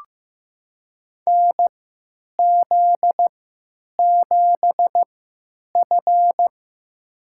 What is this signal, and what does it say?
Morse code, heard as a keyed tone, about 16 wpm, spaced as for 9 wpm, 700 Hz. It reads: NZ7F
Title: USA Call Signs - 3x Character Spacing - 15wpm